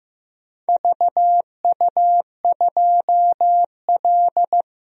Morse code, heard as a keyed tone, about 15 wpm, 700 Hz. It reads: VU2L
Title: DXCC-100 Call Signs - 4 Characters - 15wpm